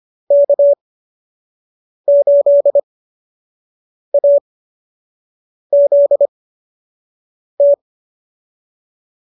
Morse code, heard as a keyed tone, about 25 wpm, spaced as for 6 wpm, 590 Hz. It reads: K8AZT